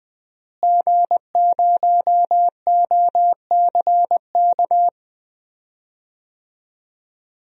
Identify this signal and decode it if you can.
Morse code, heard as a keyed tone, about 20 wpm, 700 Hz. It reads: G0OCK
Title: DXCC-100 Call Signs - 5 Characters - Repeated 2x - 20wpm